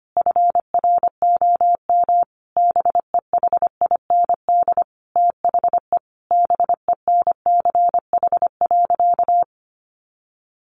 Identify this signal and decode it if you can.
Morse code, heard as a keyed tone, about 25 wpm, 700 Hz. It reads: FROM BEHIND THE BENCH.
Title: Sentences from Top 1400 Words 25wpm